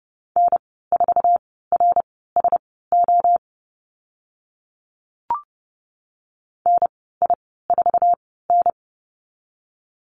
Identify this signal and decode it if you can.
Morse code, heard as a keyed tone, about 30 wpm, spaced as for 17 wpm, 700 Hz. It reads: N4RSO NI4N
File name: Lesson 14g - Number 2 - Call Signs 30wpm